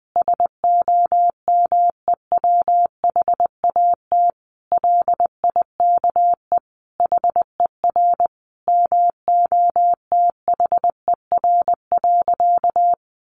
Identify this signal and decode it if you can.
Morse code, heard as a keyed tone, about 20 wpm, 700 Hz. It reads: SOMEWHAT LIKE HER MOTHER.
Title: Sentences from Top 1300 Words 20wpm